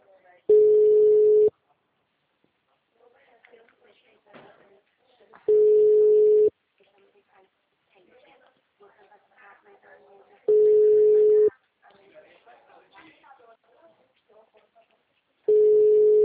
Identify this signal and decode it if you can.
Ringback tone (Europe, 425 Hz), with speech.
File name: out- -2007- - - .23797